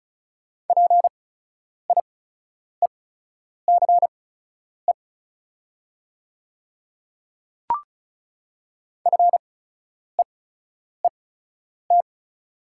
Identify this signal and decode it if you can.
Morse code, heard as a keyed tone, about 36 wpm, spaced as for 10 wpm, 700 Hz. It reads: PIECE FEET
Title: 8x Character Spacing - Top 500 Words - 35wpm